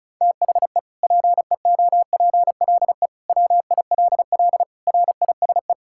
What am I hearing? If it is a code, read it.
Morse code, heard as a keyed tone, about 35 wpm, 700 Hz. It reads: THE PEOPLE WILL RISE
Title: Sentences from Top 1000 Words - Review Part 6 - 35wpm